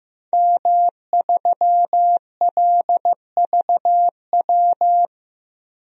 Morse code, heard as a keyed tone, about 15 wpm, 700 Hz. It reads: M3LVW
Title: DXCC-100 Call Signs - 5 Characters - Repeated 3x - 15wpm